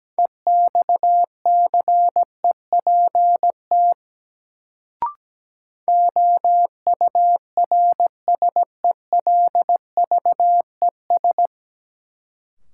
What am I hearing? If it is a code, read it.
Morse code, heard as a keyed tone, about 17 wpm, 700 Hz. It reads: EXCEPT OURSELVES
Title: Sentences from Top 1600 Words - New Word Drill - 17wpm